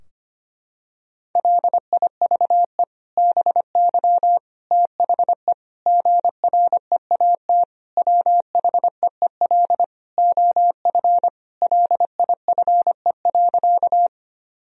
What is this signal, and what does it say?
Morse code, heard as a keyed tone, about 25 wpm, 700 Hz. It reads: LIVE BY THE GREAT WHEEL OF LIFE.
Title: Sentences from Top 1000 Words - Review Part 3 - 25wpm